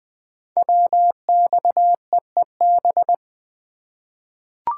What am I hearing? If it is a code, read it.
Morse code, heard as a keyed tone, about 20 wpm, 700 Hz. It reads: WXEEB
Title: Code Groups - 5 Letters - 20wpm